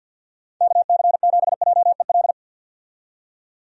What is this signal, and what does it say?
Morse code, heard as a keyed tone, about 51 wpm, 700 Hz. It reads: KC7JEL
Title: USA Call Signs - 6 Characters - 50wpm